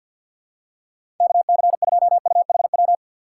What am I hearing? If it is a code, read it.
Morse code, heard as a keyed tone, about 50 wpm, 700 Hz. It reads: KC2UHW
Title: USA Call Signs - 2x Character Spacing - 50wpm